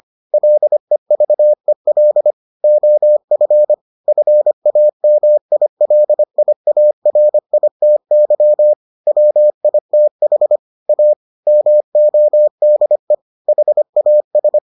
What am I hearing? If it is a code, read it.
Morse code, heard as a keyed tone, about 25 wpm, 600 Hz. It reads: LEVEL OF FAMILIARITY WITH A MODE HAS